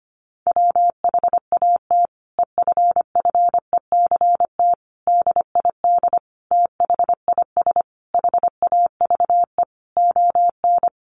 Morse code, heard as a keyed tone, about 25 wpm, 700 Hz. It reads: WHAT EFFECT DID THIS HAVE ON
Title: Sentences from Top 1000 Words 25wpm